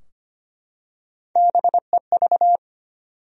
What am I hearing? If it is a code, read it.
Morse code, heard as a keyed tone, about 25 wpm, 700 Hz. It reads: BEV
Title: USA Common Names - Max 5 Letters - 25wpm